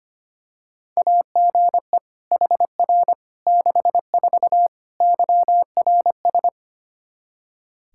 Morse code, heard as a keyed tone, about 25 wpm, 700 Hz. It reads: AGE HR 64 YRS
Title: Practice QSO - Exchange 3 - More Info - 25wpm